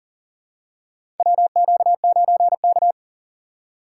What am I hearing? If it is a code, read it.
Morse code, heard as a keyed tone, about 39 wpm, 700 Hz. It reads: WQ9K
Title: USA Call Signs - 4 Characters - 3x Character Spacing - 40wpm